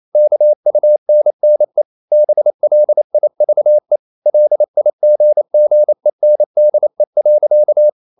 Morse code, heard as a keyed tone, about 28 wpm, 600 Hz. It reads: KUNNE BLIVE LIGGENDE.
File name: Ferie QRQ 28wpm